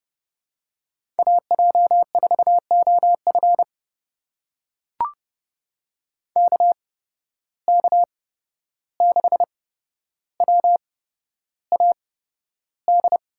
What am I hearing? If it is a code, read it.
Morse code, heard as a keyed tone, about 32 wpm, spaced as for 12 wpm, 700 Hz. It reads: AJ4OF KK6WAD
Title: USA Call Signs - 8x Character Spacing - 30wpm